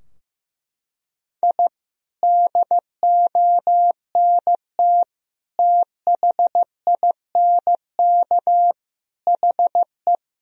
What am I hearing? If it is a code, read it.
Morse code, heard as a keyed tone, about 16 wpm, 700 Hz. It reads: I DONT THINK HE
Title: Sentences from Top 1000 Words - Review Part 4 - 15wpm